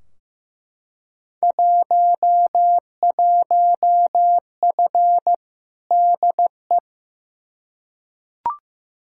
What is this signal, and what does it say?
Morse code, heard as a keyed tone, about 15 wpm, 700 Hz. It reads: 11F DE
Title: Field Day Exchanges Encore - 15wpm